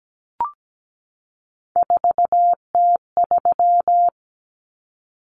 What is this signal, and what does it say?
Morse code, heard as a keyed tone, about 17 wpm, 700 Hz. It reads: 4T3
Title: Common Cut Numbers - 3-Digit 17wpm